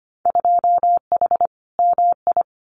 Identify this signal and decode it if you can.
Morse code, heard as a keyed tone, about 25 wpm, 700 Hz. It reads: 2H MI
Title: Winter Field Day Exchanges Encore 25wpm